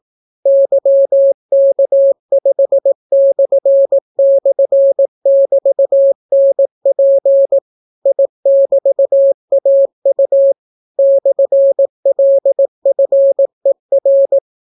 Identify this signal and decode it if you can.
Morse code, heard as a keyed tone, about 18 wpm, 550 Hz. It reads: YK5//<BT>NP I<BT>AU /LFER